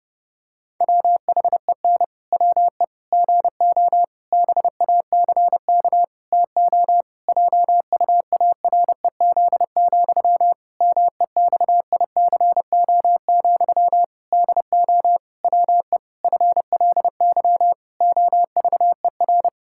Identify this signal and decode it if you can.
Morse code, heard as a keyed tone, about 30 wpm, 700 Hz. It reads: WHEN WE GO BACK TO JUAREZ, MEXICO, DO WE FLY OVER
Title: Pangram Challenge v2 - 30wpm